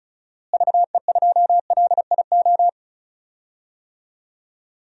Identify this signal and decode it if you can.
Morse code, heard as a keyed tone, about 36 wpm, 700 Hz. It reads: VE2LIO
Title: DXCC-100 Call Signs - 6 Characters - Repeated 2x - 35wpm